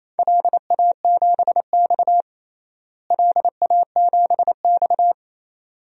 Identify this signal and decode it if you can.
Morse code, heard as a keyed tone, about 28 wpm, 700 Hz. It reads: LA7X LA7X
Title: DXCC-100 Call Signs - 4 Characters - Repeated 2x - 28wpm